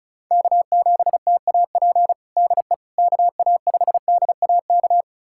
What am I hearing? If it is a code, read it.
Morse code, heard as a keyed tone, about 36 wpm, 690 Hz. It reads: K7TAP DE KA5DAK
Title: USA Callsign DE Callsign 35wpm